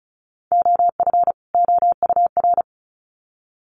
Morse code, heard as a keyed tone, about 34 wpm, 700 Hz. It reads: OF OUR
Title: Sets of 2 Words 35wpm